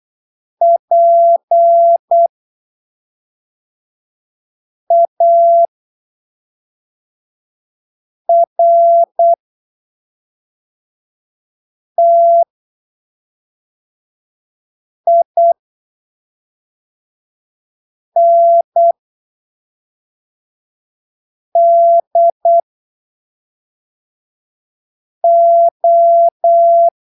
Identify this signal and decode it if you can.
Morse code, heard as a keyed tone, about 8 wpm, spaced as for 3 wpm, 680 Hz. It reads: PARTINDO
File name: arquivos / texto / texto 08 08ppm